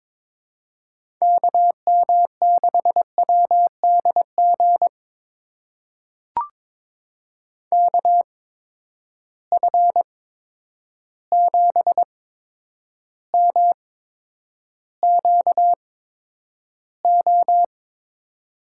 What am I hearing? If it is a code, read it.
Morse code, heard as a keyed tone, about 22 wpm, spaced as for 9 wpm, 700 Hz. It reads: KM6WDG KF7MQO
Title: USA Call Signs - 6 Characters - 8x Character Spacing - 22wpm